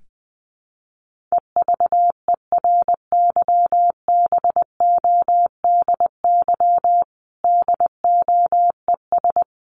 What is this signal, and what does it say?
Morse code, heard as a keyed tone, about 20 wpm, 700 Hz. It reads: EVERYBODY DOES
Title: Sentences from Top 1400 Words 20wpm